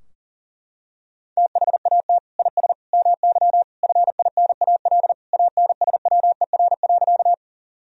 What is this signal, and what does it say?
Morse code, heard as a keyed tone, about 40 wpm, 700 Hz. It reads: THAT IS MY FINAL ANSWER.
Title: Sentences from Top 1000 Words - Review Part 2 - 40wpm